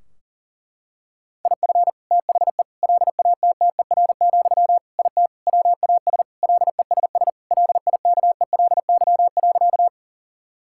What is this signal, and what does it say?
Morse code, heard as a keyed tone, about 37 wpm, 700 Hz. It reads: IF THE LATTER, IT WAS LESS LIKELY.
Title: Sentences from Top 1600 Words 40wpm